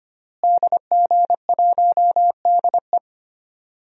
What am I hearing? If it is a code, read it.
Morse code, heard as a keyed tone, about 25 wpm, 700 Hz. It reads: DG1DE
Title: DXCC-100 Call Signs - 5 Characters - Repeated 3x - 25wpm